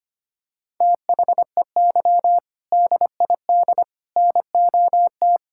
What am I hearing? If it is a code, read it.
Morse code, heard as a keyed tone, about 25 wpm, 700 Hz. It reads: THEY DID NOT
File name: practice / 25wpm / Sentences with 3-Word Phrases - 25wpm